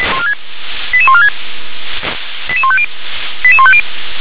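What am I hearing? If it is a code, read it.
Five-tone selective call (ZVEI), heard as quick runs of five tones, tones of about 70 ms. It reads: 88146 88150 88150
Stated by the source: Five-tone ZVEI selective call received on SDR.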